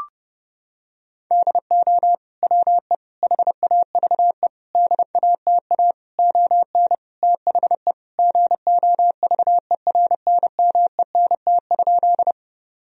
Morse code, heard as a keyed tone, about 30 wpm, 700 Hz. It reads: DO WE HAVE DATA ON THE GOVERNMENT?
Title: Sentences from Top 1000 Words - Review Part 3 - 30wpm